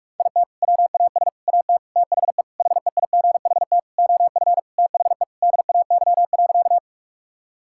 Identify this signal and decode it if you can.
Morse code, heard as a keyed tone, about 41 wpm, 700 Hz. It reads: IT WAS AT THE HEIGHT OF THE DAY.
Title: Sentences from Top 1600 Words 45wpm